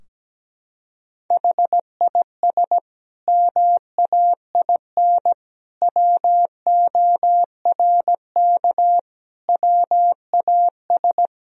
Morse code, heard as a keyed tone, about 17 wpm, 700 Hz. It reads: HIS MAIN WORK WAS
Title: Sentences from Top 1000 Words - Review Part 2 - 17wpm